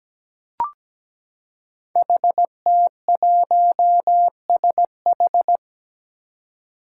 Morse code, heard as a keyed tone, about 17 wpm, 700 Hz. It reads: HT1SH
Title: Five Challenging Letter-Number Combinations 17wpm